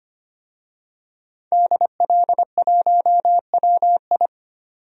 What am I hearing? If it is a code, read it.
Morse code, heard as a keyed tone, about 25 wpm, 700 Hz. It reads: DL1WI
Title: DXCC-100 Call Signs - 5 Characters - 25wpm